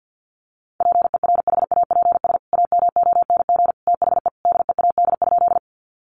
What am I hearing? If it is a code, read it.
Morse code, heard as a keyed tone, about 50 wpm, 700 Hz. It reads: PERHAPS AMONG THE DEAD?
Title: Sentences from Top 1500 Words 50wpm